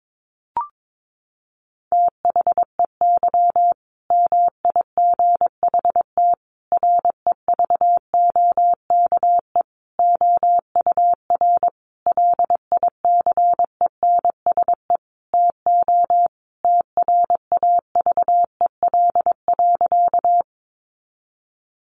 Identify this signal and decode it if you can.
Morse code, heard as a keyed tone, about 22 wpm, 700 Hz. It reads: THEY MIGHT REVOKE OUR LICENSE TO TRAVEL.
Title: A Sound Of Thunder 22wpm